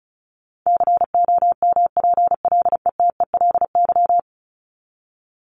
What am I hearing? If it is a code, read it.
Morse code, heard as a keyed tone, about 34 wpm, 700 Hz. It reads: COMPLETELY
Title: LY Suffix - Top 100 Words with Base Words 35wpm